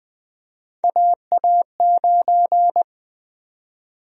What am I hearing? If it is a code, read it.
Morse code, heard as a keyed tone, about 20 wpm, 700 Hz. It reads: AA9
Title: USA Call Sign Prefixes - 20wpm